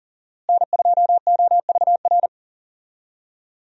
Morse code, heard as a keyed tone, about 40 wpm, 700 Hz. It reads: N2OVR
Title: USA Call Signs - 5 Characters - Repeated 3x - 40wpm